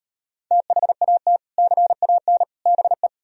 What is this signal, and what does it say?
Morse code, heard as a keyed tone, about 38 wpm, 700 Hz. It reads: THAT CAN BE